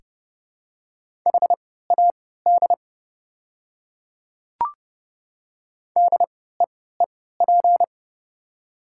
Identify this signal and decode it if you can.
Morse code, heard as a keyed tone, about 30 wpm, spaced as for 17 wpm, 700 Hz. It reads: HAD DEEP
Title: 3x Character Spacing - Top 500 Words - 30wpm